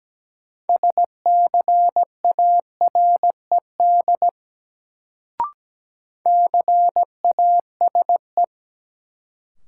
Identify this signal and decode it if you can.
Morse code, heard as a keyed tone, about 17 wpm, 700 Hz. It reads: SCARED CASE
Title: Sentences from Top 700 Words - New Word Drill - 17wpm